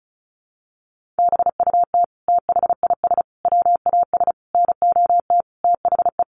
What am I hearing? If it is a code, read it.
Morse code, heard as a keyed tone, about 34 wpm, 700 Hz. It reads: BUT THIS WAS NOT THE
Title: Sentences from Top 700 Words 35wpm